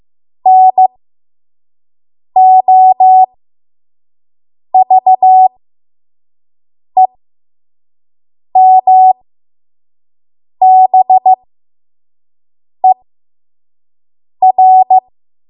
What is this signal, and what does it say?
Morse code, heard as a keyed tone, about 15 wpm, spaced as for 5 wpm, 750 Hz. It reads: NOVEMBER
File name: Morse / 05WPM